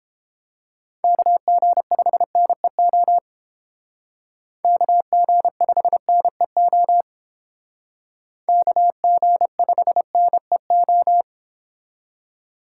Morse code, heard as a keyed tone, about 29 wpm, 700 Hz. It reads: KG5NEO KG5NEO KG5NEO